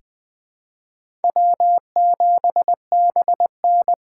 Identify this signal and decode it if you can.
Morse code, heard as a keyed tone, about 20 wpm, 700 Hz. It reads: W7BN
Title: USA Call Signs - 4 Characters - 4x Character Spacing - 20wpm